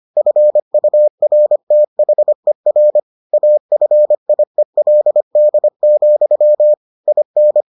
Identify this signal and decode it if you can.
Morse code, heard as a keyed tone, about 25 wpm, 600 Hz. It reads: FURTHER AFIELD, IN